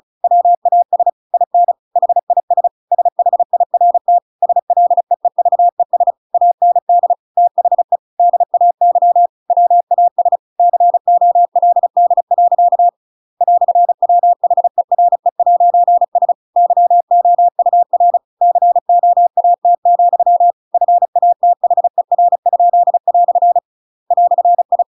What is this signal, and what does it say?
Morse code, heard as a keyed tone, about 34 wpm, 700 Hz. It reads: WAS IN HIS SHIRT SLEEVES AND THE DAY WAS COLD. "WHERE'S YOUR COAT, FATHER?" "I
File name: Pinocchio / Pinocchio ch8 35 35 ik1hge